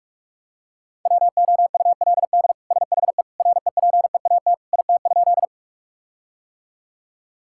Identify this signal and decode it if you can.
Morse code, heard as a keyed tone, about 41 wpm, 700 Hz. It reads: WOULD SHE REPEAT IT?
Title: Sentences from Top 1600 Words 45wpm